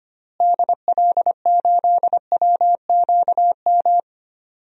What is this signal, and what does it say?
Morse code, heard as a keyed tone, about 25 wpm, 700 Hz. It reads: DL8WQM
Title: DXCC-100 Call Signs - 6 Characters - Repeated 2x - 25wpm